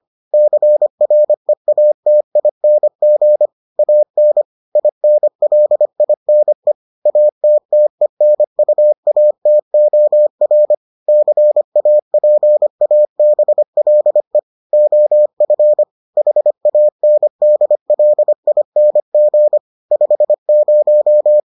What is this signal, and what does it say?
Morse code, heard as a keyed tone, about 25 wpm, 600 Hz. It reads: CREATING AN INLINE ATTENUATOR CAPABLE OF HANDLING 50